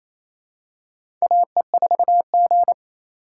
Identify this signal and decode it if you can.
Morse code, heard as a keyed tone, about 28 wpm, 700 Hz. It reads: AE4G
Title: USA Call Signs - 4x Character Spacing - 28wpm